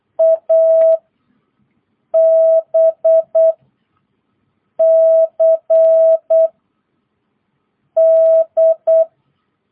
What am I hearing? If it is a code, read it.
Morse code, heard as a keyed tone, about 8 wpm, 650 Hz. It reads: A B C D
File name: jh8cbh / 2022cw